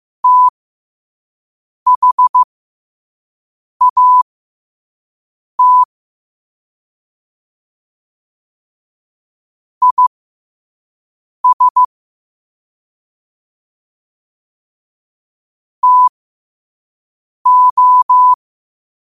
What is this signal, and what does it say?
Morse code, heard as a keyed tone, about 15 wpm, spaced as for 5 wpm, 1000 Hz. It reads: THAT IS TO
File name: qotdfiles / 5wpm